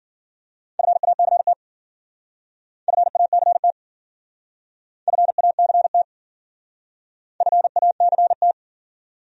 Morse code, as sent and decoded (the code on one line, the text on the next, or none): ..-. .- -.-. - / ..-. .- -.-. - / ..-. .- -.-. - / ..-. .- -.-. -
FACT FACT FACT FACT